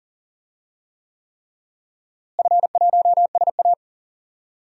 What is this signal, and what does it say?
Morse code, heard as a keyed tone, about 39 wpm, 700 Hz. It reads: F1SA
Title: DXCC-100 Call Signs - 4 Characters - Repeated 3x - 40wpm